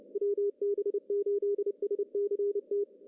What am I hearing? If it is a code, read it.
Morse code, heard as a keyed tone, about 30 wpm, 400 Hz. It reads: WB8SCT
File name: CW / cqww-cw-k5tr